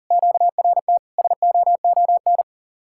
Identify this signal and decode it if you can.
Morse code, heard as a keyed tone, about 50 wpm, 700 Hz. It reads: QRT SOON
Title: Speed-Racing Common QSO Phrases Encore - 40wpm